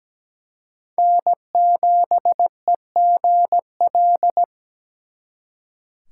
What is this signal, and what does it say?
Morse code, heard as a keyed tone, about 20 wpm, 700 Hz. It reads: N7EGL
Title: Speed Racing - USA 1x3 Call Signs 17wpm